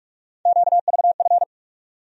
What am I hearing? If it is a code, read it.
Morse code, heard as a keyed tone, about 46 wpm, 700 Hz. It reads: QVF
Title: USA Call Sign Postfixes - 3 Letters 45wpm